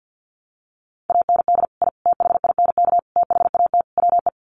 Morse code, heard as a keyed tone, about 50 wpm, 700 Hz. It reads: AND I THINK THAT WE
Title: Sets of 5 Words Encore2 50wpm